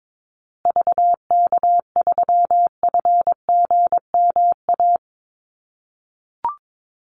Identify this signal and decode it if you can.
Morse code, heard as a keyed tone, about 22 wpm, 700 Hz. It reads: VK3FGMA
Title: USA Call Signs Repeated 2x 22wpm